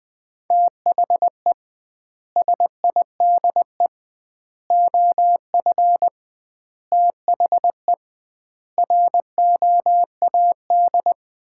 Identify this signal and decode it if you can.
Morse code, heard as a keyed tone, about 21 wpm, 700 Hz. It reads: THE SIDE OF THE ROAD
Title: 2x Word-Spacing - Sets of 5 Words - 20wpm